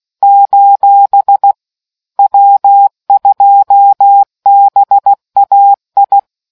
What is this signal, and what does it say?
Morse code, heard as a keyed tone, about 16 wpm, 800 Hz. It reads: : W2BAI